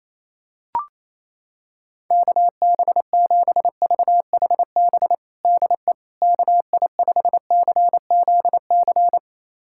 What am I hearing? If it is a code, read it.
Morse code, heard as a keyed tone, about 28 wpm, 700 Hz. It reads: KB7VHB DE KI5CZC
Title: USA Callsign DE Callsign - 6 Characters - 28wpm